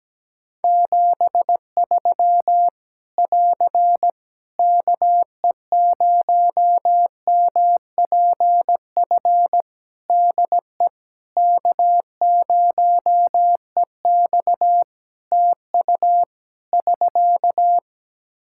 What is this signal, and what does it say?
Morse code, heard as a keyed tone, about 17 wpm, 700 Hz. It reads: 73 <AR> KE0MPF DE K0EX TU <SK>
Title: Practice QSO - Exchange 4 - Ending - 17wpm